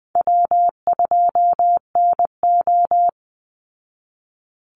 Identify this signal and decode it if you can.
Morse code, heard as a keyed tone, about 20 wpm, 700 Hz. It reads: W2NO
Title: Call Signs Repeated 3x - 20wpm